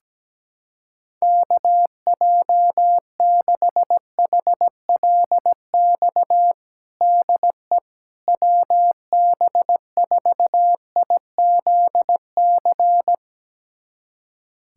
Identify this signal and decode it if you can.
Morse code, heard as a keyed tone, about 17 wpm, 700 Hz. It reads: KJ6HLX DE WB4IZC
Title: USA Callsign DE Callsign - 6 Characters - 17wpm